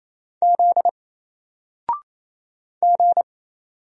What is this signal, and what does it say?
Morse code, heard as a keyed tone, about 27 wpm, 700 Hz. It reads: Z G
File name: Single-Letter-Number-28wpm